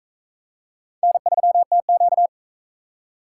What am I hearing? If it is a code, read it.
Morse code, heard as a keyed tone, about 43 wpm, 700 Hz. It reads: N3TQ